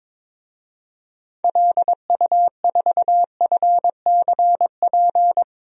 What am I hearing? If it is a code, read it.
Morse code, heard as a keyed tone, about 22 wpm, 700 Hz. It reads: LU4FCP
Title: DXCC-100 Call Signs - 6 Characters - Repeated 2x - 22wpm